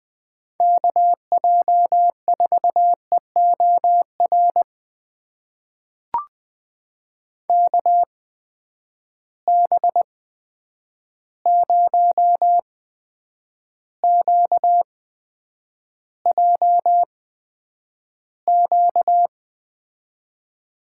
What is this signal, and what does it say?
Morse code, heard as a keyed tone, about 20 wpm, spaced as for 9 wpm, 700 Hz. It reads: KJ4EOR KB0QJQ
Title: USA Call Signs - 6 Characters - 8x Character Spacing - 20wpm